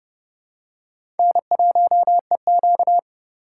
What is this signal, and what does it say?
Morse code, heard as a keyed tone, about 30 wpm, 700 Hz. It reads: N1EQ
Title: USA Call Signs - 3x Character Spacing - 30wpm